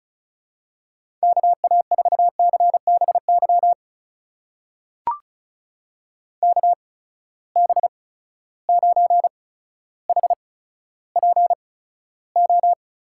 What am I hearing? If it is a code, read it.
Morse code, heard as a keyed tone, about 36 wpm, spaced as for 15 wpm, 700 Hz. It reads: KA4CBY KB9HPO
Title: USA Call Signs - 6 Characters - 8x Character Spacing - 35wpm